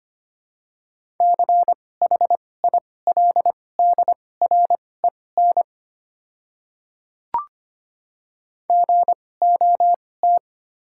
Morse code, heard as a keyed tone, about 25 wpm, spaced as for 18 wpm, 700 Hz. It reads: CHILDREN GOT
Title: 2x Character Spacing - Top 300 Words - 25wpm